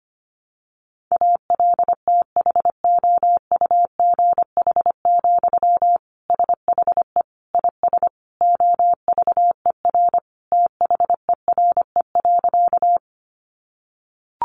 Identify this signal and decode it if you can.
Morse code, heard as a keyed tone, about 25 wpm, 700 Hz. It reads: ALTHOUGH, SHE IS OVER THERE.
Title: Sentences from Top 600 Words 25wpm